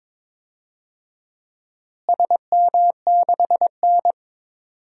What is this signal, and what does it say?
Morse code, heard as a keyed tone, about 22 wpm, 700 Hz. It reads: SM6N
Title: DXCC-100 Call Signs - 4 Characters - Repeated 2x - 22wpm